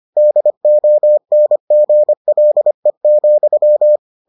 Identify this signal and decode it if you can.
Morse code, heard as a keyed tone, about 25 wpm, 600 Hz. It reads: DONGLE,